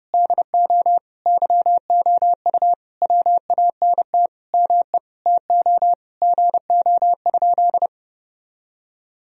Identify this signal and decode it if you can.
Morse code, heard as a keyed tone, about 30 wpm, 700 Hz. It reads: DO YOU WANT ME TO GO?
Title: Sentences from Top 1000 Words - Review Part 3 - 30wpm